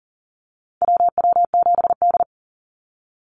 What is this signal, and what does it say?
Morse code, heard as a keyed tone, about 40 wpm, 700 Hz. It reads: WW7D
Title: NA SOTA Chaser Callsigns 40wpm